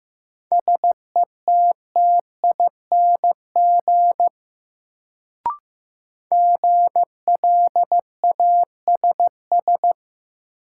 Morse code, heard as a keyed tone, about 15 wpm, 700 Hz. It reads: SETTING GLASS
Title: Sentences from Top 1300 Words - New Word Drill Alt - 15wpm